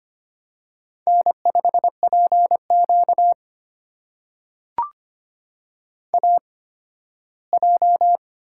Morse code, heard as a keyed tone, about 25 wpm, spaced as for 15 wpm, 700 Hz. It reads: N5PQ AJ
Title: USA Call Signs - 4 Characters - 8x Character Spacing - 25wpm